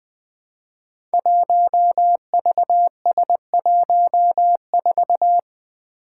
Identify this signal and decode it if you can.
Morse code, heard as a keyed tone, about 20 wpm, 700 Hz. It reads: 1VS14
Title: Five Challenging Letter-Number Combinations 20wpm